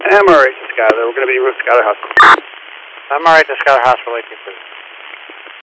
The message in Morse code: none

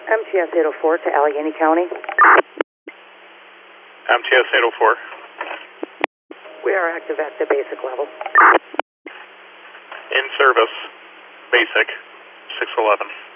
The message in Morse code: none

